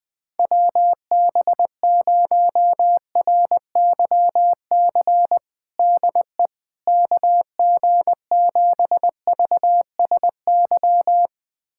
Morse code, heard as a keyed tone, about 20 wpm, 700 Hz. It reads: WB0RYC DE KG7VSY
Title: USA Callsign DE Callsign 20wpm